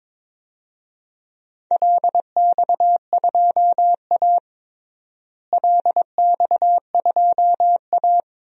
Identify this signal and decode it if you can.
Morse code, heard as a keyed tone, about 22 wpm, 700 Hz. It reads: LX2A LX2A
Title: DXCC-100 Call Signs - 4 Characters - Repeated 2x - 22wpm